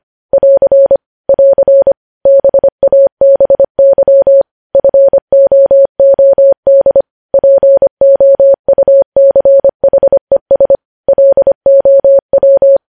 Morse code, heard as a keyed tone, about 25 wpm, 550 Hz. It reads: <AR> <AR> BABY FOOD POUCHES LOW
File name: podcast / News headlines-1x-25x25